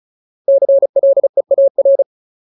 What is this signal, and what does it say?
Morse code, heard as a keyed tone, about 34 wpm, 550 Hz. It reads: CLEAR
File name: Ultimate Word List / 35WPM Ultimate